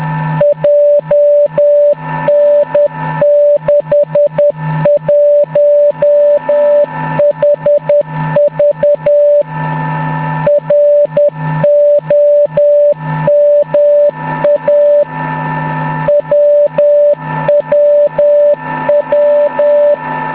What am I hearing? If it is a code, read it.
Morse code, heard as a keyed tone, about 10 wpm, 580 Hz. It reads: JN61HV ROMA WWW